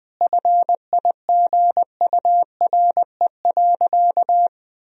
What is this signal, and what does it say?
Morse code, heard as a keyed tone, about 21 wpm, 700 Hz. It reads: FIGURE.
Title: Sentences from Top 1000 Words - Review Part 2 - 20wpm